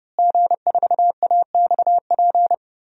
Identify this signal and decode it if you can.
Morse code, heard as a keyed tone, about 30 wpm, 700 Hz. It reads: G4AXP
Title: DXCC-100 Call Signs - 5 Characters - Repeated 2x - 30wpm